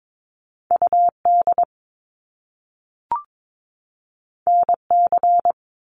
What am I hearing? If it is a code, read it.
Morse code, heard as a keyed tone, about 22 wpm, 700 Hz. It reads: UD NC
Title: Top Two Letter Combinations in Words 22wpm